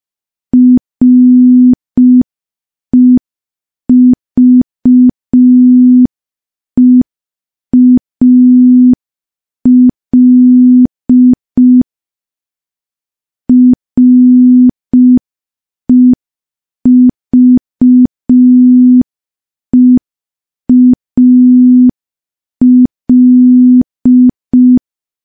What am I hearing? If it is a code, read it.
Morse code, heard as a keyed tone, about 5 wpm, 250 Hz. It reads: REVEAL REVEAL
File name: morse / morse-training-05WPM